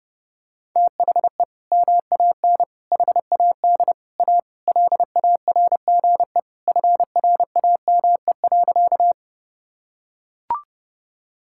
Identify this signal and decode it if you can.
Morse code, heard as a keyed tone, about 30 wpm, 700 Hz. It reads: THE MAN HAD A LARGE FRAME.
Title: Sentences from Top 1000 Words - Review Part 6 - 30wpm